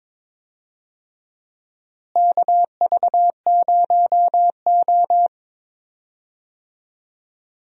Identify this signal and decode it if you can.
Morse code, heard as a keyed tone, about 22 wpm, 700 Hz. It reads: KV0O
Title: USA Call Signs - 4 Characters - Repeated 2x - 22wpm